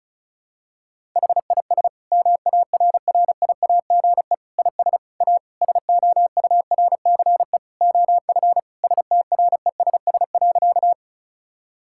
Morse code, heard as a keyed tone, about 36 wpm, 700 Hz. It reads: HIS MARRIAGE IS A SOURCE OF STRESS.
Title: Sentences from Top 1000 Words - Review Part 6 - 35wpm